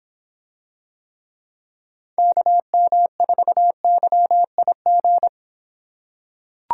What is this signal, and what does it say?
Morse code, heard as a keyed tone, about 26 wpm, 700 Hz. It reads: KM4YIG